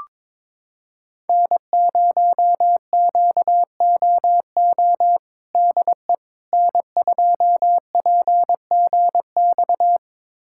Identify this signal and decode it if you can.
Morse code, heard as a keyed tone, about 21 wpm, 700 Hz. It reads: N0QOO DE N2PGX
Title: USA Callsign DE Callsign - 5 Characters - 22wpm